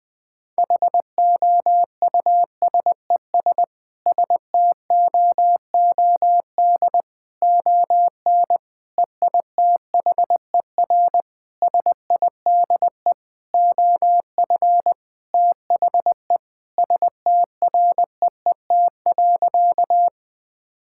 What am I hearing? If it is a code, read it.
Morse code, heard as a keyed tone, about 20 wpm, 700 Hz. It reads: HOUSES STOOD ON EITHER SIDE OF THE STREET.
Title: Sentences with 5-Word Phrases - Spoken Beginning - 20wpm